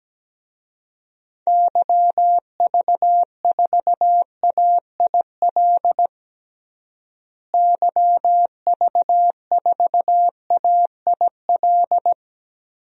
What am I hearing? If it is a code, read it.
Morse code, heard as a keyed tone, about 17 wpm, 700 Hz. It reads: YV4AIL YV4AIL
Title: DXCC-100 Call Signs - 6 Characters - Repeated 2x - 17wpm